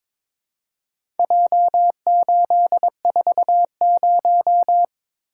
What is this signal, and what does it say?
Morse code, heard as a keyed tone, about 22 wpm, 700 Hz. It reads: J840
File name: practice / 22wpm / Four Letter-Number Combinations - 22wpm